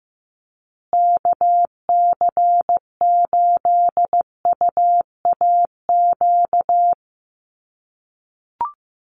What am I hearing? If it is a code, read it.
Morse code, heard as a keyed tone, about 15 wpm, 700 Hz. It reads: KC8UAQ
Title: USA Call Signs Repeated 3x 15wpm